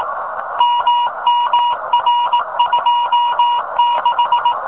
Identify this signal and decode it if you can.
Morse code, heard as a keyed tone, about 18 wpm, 1000 Hz, with noise. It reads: MMR26
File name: rs15 telem